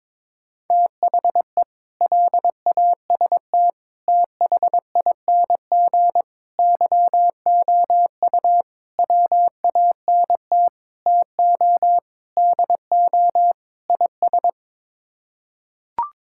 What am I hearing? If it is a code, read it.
Morse code, heard as a keyed tone, about 22 wpm, 700 Hz. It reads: THE LAST THING YOU WANT TO DO IS
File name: practice / 22wpm / Sets of 8 Words - 22wpm